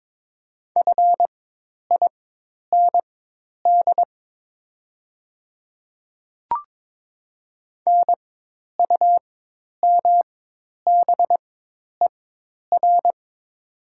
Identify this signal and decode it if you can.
Morse code, heard as a keyed tone, about 22 wpm, spaced as for 10 wpm, 700 Hz. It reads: FIND NUMBER
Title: 4x Character Spacing - Top 100 Words - 22wpm